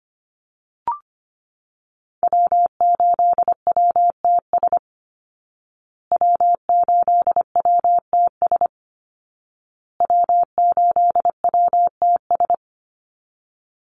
Morse code, heard as a keyed tone, about 25 wpm, 700 Hz. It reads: W8WTS W8WTS W8WTS
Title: Call Signs Repeated 2x - 25wpm